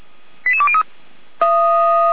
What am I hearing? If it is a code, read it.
Five-tone selective call (ZVEI), heard as quick runs of five tones, tones of about 70 ms. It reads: 88203
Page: /455/SDR/ZVEI/